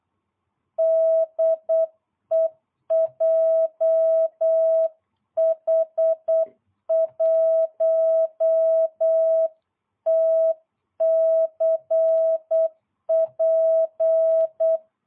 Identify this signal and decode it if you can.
Morse code, heard as a keyed tone, about 8 wpm, 660 Hz. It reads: DE JH1TCP